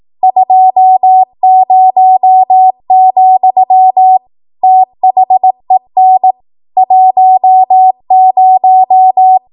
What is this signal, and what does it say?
Morse code, heard as a keyed tone, about 18 wpm, 750 Hz. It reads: 20, THEN 10
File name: Archive / 18 WPM / 18WPM